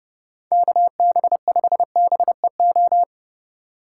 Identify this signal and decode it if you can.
Morse code, heard as a keyed tone, about 28 wpm, 700 Hz. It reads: KB5BEO
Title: Speed Racing - USA 2x3 Call Signs 30wpm